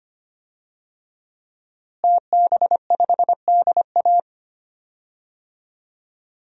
Morse code, heard as a keyed tone, about 25 wpm, 700 Hz. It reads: TB5DA
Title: DXCC-100 Call Signs - 5 Characters - Repeated 2x - 25wpm